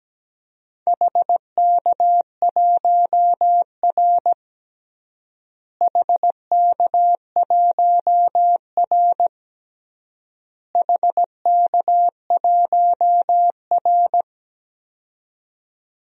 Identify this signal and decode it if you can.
Morse code, heard as a keyed tone, about 17 wpm, 700 Hz. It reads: HK1R HK1R HK1R
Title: DXCC-100 Call Signs - 4 Characters - Repeated 3x - 17wpm